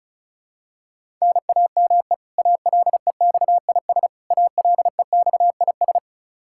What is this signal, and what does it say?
Morse code, heard as a keyed tone, about 36 wpm, 700 Hz. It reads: NAME ALEXIS ALEXIS
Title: K1USN SST QSO Practice 35wpm